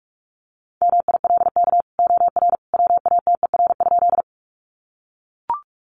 Morse code, heard as a keyed tone, about 44 wpm, 700 Hz. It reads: MILK OR WATER?
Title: Sentences from Top 1500 Words 45wpm